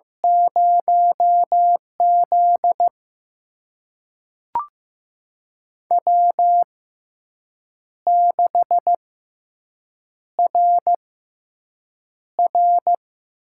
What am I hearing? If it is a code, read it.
Morse code, heard as a keyed tone, about 15 wpm, spaced as for 6 wpm, 700 Hz. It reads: 0Z W6RR
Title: USA Call Signs - 4 Characters - 6x Character Spacing - 15wpm